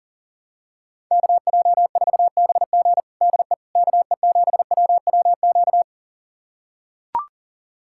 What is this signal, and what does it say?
Morse code, heard as a keyed tone, about 40 wpm, 700 Hz. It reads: KJ4BG DE KE7WWQ
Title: USA Callsign DE Callsign 40wpm